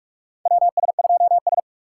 Morse code, heard as a keyed tone, about 46 wpm, 690 Hz. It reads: WS2S
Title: USA Call Signs - 4x Character Spacing - 45wpm